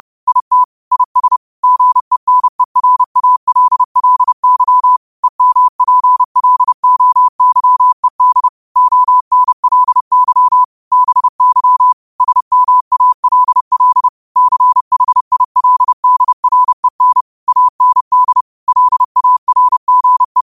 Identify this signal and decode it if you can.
Morse code, heard as a keyed tone, about 30 wpm, 1000 Hz. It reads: IT IS GENERALLY EMPLOYED ONLY BY SMALL CHILDREN AND LARGE
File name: 30wpm